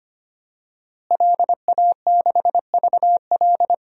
Morse code, heard as a keyed tone, about 25 wpm, 700 Hz. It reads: LA6VL
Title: DXCC-100 Call Signs - 5 Characters - Repeated 3x - 25wpm